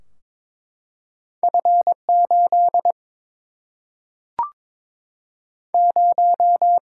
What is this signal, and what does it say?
Morse code, heard as a keyed tone, about 22 wpm, 700 Hz. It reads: F 8 0
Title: Single-Letter Single-Number 22wpm